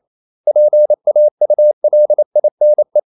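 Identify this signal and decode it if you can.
Morse code, heard as a keyed tone, about 28 wpm, 600 Hz. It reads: PAULINE
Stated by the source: Morse code practice at 28 wpm, text of Ken Follett's novel Never in English, from CW med Gnister.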